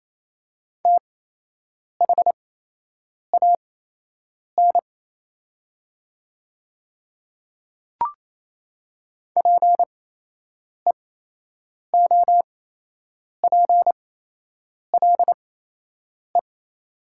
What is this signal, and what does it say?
Morse code, heard as a keyed tone, about 28 wpm, spaced as for 8 wpm, 700 Hz. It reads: THAN PEOPLE
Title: 8x Character Spacing - Top 100 Words - 28wpm